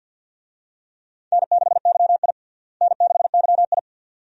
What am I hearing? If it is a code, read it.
Morse code, heard as a keyed tone, about 47 wpm, 700 Hz. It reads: N6YI N6YI
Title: USA Call Signs - 4 Characters - Repeated 2x - 50wpm